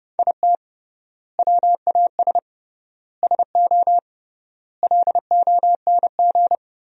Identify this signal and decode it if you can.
Morse code, heard as a keyed tone, about 30 wpm, 700 Hz. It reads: IT WAS SO LONG
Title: 3x Word-Spacing - Sentences from Top 500 Words - 30wpm